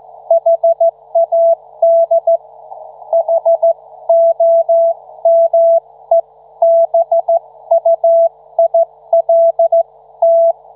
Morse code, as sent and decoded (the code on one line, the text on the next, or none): .... .- -.. / .... --- -- . -... ..- .. .-.. -
HAD HOMEBUILT